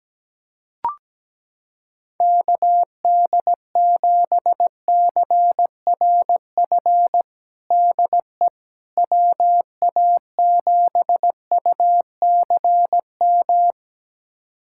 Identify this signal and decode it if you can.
Morse code, heard as a keyed tone, about 17 wpm, 700 Hz. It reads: KD7CRF DE WA7UCM
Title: USA Callsign DE Callsign - 6 Characters - 17wpm